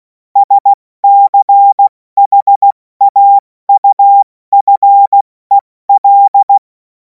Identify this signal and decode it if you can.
Morse code, heard as a keyed tone, about 16 wpm, 800 Hz. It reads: SCHAUFEL
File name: Morsekurs / Witze / Witz 17. Februar 2026 70 F80